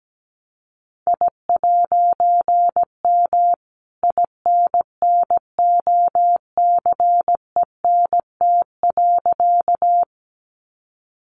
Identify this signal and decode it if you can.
Morse code, heard as a keyed tone, about 17 wpm, 700 Hz. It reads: I'M INNOCENT.
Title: A Sound Of Thunder 17wpm